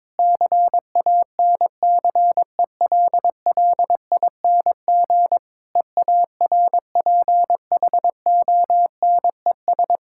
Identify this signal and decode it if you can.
Morse code, heard as a keyed tone, about 22 wpm, 690 Hz. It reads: CANCELLING EARPHONES
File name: Difficult Words with Hints - 22wpm